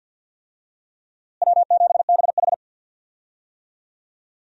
Morse code, heard as a keyed tone, about 52 wpm, 700 Hz. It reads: W7BH
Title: USA Call Signs - 4 Characters - Repeated 2x - 50wpm